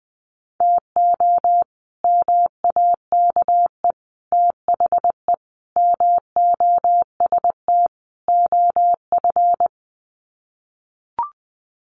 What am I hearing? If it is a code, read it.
Morse code, heard as a keyed tone, about 19 wpm, 700 Hz. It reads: TO MAKE THE MOST OF
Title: Sets of 5 Words Encore2 20wpm